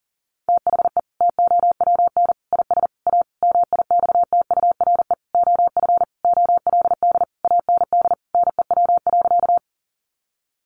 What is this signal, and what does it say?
Morse code, heard as a keyed tone, about 40 wpm, 700 Hz. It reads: THE TOWN IS A MIXTURE OF OLD AND NEW.
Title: Sentences from Top 1500 Words 40wpm